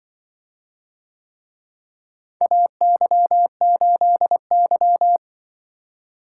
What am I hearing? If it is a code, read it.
Morse code, heard as a keyed tone, about 25 wpm, 700 Hz. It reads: AY8Y
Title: Speed-Racing - Four Letter-Number Combinations - 20wpm